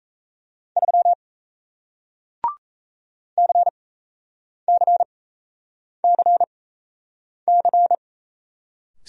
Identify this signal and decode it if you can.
Morse code, heard as a keyed tone, about 38 wpm, spaced as for 8 wpm, 700 Hz. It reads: 3 CCCC